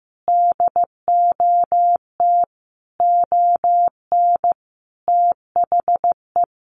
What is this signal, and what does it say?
Morse code, heard as a keyed tone, about 15 wpm, 700 Hz. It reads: DOT ON THE
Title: Sentences from Top 1500 Words 15wpm